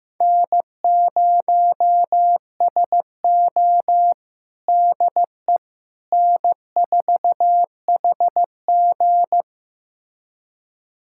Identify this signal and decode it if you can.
Morse code, heard as a keyed tone, about 15 wpm, 700 Hz. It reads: N0SO DE N4HG
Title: USA Callsign DE Callsign - 4 Characters - 15wpm